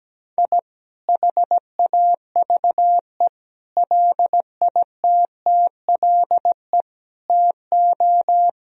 Morse code, heard as a keyed tone, about 17 wpm, 700 Hz. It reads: I HAVE LITTLE TO
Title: Sentences from Top 1000 Words - Review Part 1 - 17wpm